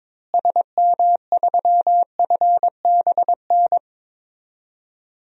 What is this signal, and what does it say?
Morse code, heard as a keyed tone, about 21 wpm, 700 Hz. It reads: SM3FBN
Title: DXCC-100 Call Signs - 6 Characters - Repeated 2x - 22wpm